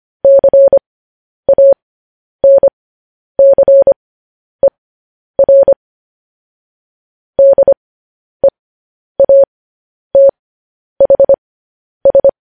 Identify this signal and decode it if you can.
Morse code, heard as a keyed tone, about 25 wpm, spaced as for 10 wpm, 550 Hz. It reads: CANCER DEATHS
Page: news-headlines-in-morse-code-at-10-wpm